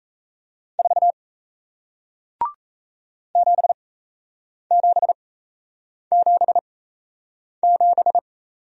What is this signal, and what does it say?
Morse code, heard as a keyed tone, about 38 wpm, spaced as for 8 wpm, 700 Hz. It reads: 4 7777